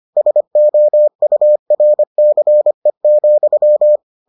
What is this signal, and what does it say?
Morse code, heard as a keyed tone, about 25 wpm, 600 Hz. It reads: SOURCE,